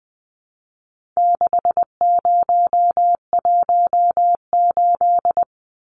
Morse code, heard as a keyed tone, about 20 wpm, 700 Hz. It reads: 6018
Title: Numbers - 4-Digit 20wpm